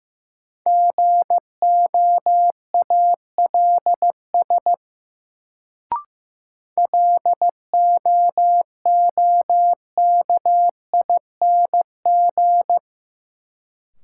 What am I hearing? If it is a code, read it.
Morse code, heard as a keyed tone, about 16 wpm, 700 Hz. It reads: GOALS LOOKING
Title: Sentences from Top 1000 Words - Review Part 6 - New Word Drill - 15wpm